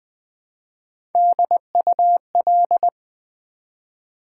Morse code, heard as a keyed tone, about 20 wpm, 700 Hz. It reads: DUL
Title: Build A Word - 1 Letter At A Time - 3 to 5-Letter Words - 20wpm